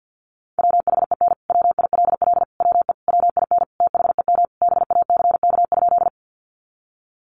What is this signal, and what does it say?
Morse code, heard as a keyed tone, about 50 wpm, 700 Hz. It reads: WHEN WILL WE WIN THEM BACK?
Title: Sentences from Top 1500 Words 50wpm